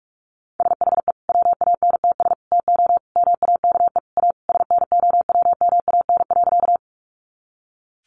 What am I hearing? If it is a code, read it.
Morse code, heard as a keyed tone, about 44 wpm, 700 Hz. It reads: SHE WANTS TO MAKE A SNOWMAN.
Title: Easy Phrases 45wpm